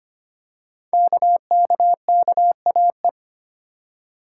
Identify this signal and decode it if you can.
Morse code, heard as a keyed tone, about 25 wpm, 700 Hz. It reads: KKKAE
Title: Code Groups - 5 Letters - 25wpm